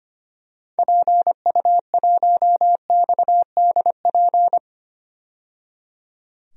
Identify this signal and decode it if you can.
Morse code, heard as a keyed tone, about 25 wpm, 700 Hz. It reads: PU1XDP
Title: DXCC-100 Call Signs - 6 Characters - 25wpm